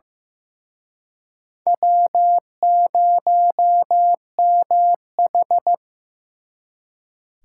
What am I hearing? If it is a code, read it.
Morse code, heard as a keyed tone, about 18 wpm, 700 Hz. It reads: W0MH
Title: Speed Racing - USA 1x2 Call Signs 15wpm